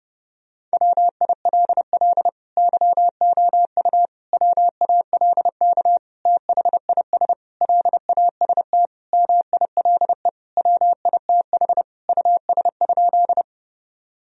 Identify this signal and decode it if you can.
Morse code, heard as a keyed tone, about 30 wpm, 700 Hz. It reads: WILL YOU WALK THIS LAST MILE WITH US?
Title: Sentences from Top 1000 Words - Review Part 1 - 30wpm